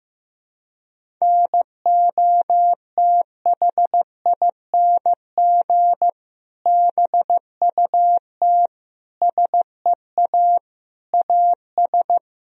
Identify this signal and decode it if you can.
Morse code, heard as a keyed tone, about 15 wpm, 700 Hz. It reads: NOTHING BUT SEA AS
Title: Sentences from Top 1000 Words - Review Part 1 - 15wpm